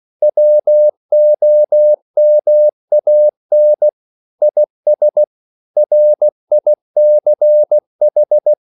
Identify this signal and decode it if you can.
Morse code, heard as a keyed tone, about 16 wpm, 600 Hz. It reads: WOMAN IS RICH